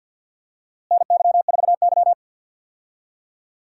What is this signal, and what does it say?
Morse code, heard as a keyed tone, about 50 wpm, 700 Hz. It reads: NY4Y
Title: USA Call Signs - 4 Characters - Repeated 2x - 50wpm